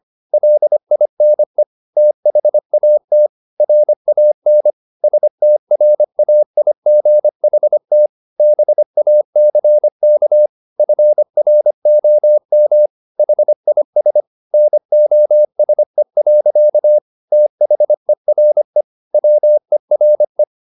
Morse code, heard as a keyed tone, about 25 wpm, 600 Hz. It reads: LINE THAT RAN STRAIGHT BACK FROM HIS NOSE. THERE WERE